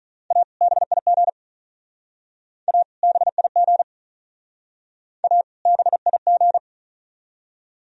Morse code, heard as a keyed tone, about 39 wpm, 700 Hz. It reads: A BIG A BIG A BIG
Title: Speed-Racing - Sets of 2 Words 35wpm